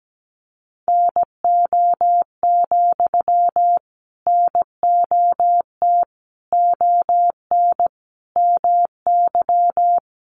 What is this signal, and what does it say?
Morse code, heard as a keyed tone, about 17 wpm, 700 Hz. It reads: NO, NOT ON MY
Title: Sentences from Top 900 Words 17wpm